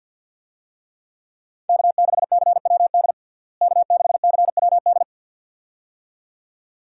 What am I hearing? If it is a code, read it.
Morse code, heard as a keyed tone, about 50 wpm, 700 Hz. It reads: K6CWD K6CWD
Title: USA Call Signs - 5 Characters - Repeated 2x - 50wpm